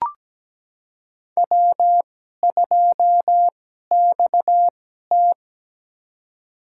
Morse code, heard as a keyed tone, about 17 wpm, spaced as for 12 wpm, 700 Hz. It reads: W2XT
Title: USA Call Signs - 2x Character Spacing - 17wpm